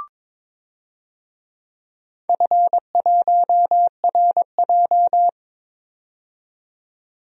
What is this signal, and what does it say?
Morse code, heard as a keyed tone, about 22 wpm, 690 Hz. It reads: F1RJ
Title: DXCC-100 Call Signs - 4 Characters - 22wpm